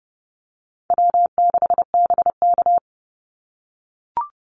Morse code, heard as a keyed tone, about 30 wpm, 700 Hz. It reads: W6BK
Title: Easy Call Signs - 30wpm